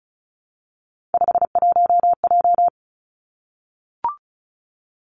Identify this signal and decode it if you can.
Morse code, heard as a keyed tone, about 36 wpm, 700 Hz. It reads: 51J
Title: Three Challenging Letter-Number Combinations 35wpm